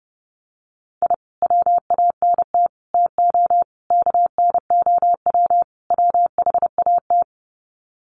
Morse code, heard as a keyed tone, about 30 wpm, 700 Hz. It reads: I WANT TO KNOW WHAT
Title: Sets of 5 Words Encore2 30wpm